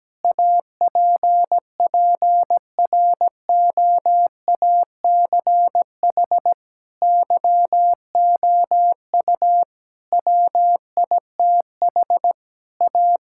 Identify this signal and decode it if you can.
Morse code, heard as a keyed tone, about 17 wpm, 700 Hz. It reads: APPROACH YOU WITH A
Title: Sentences from Top 1000 Words - Review Part 6 - 17wpm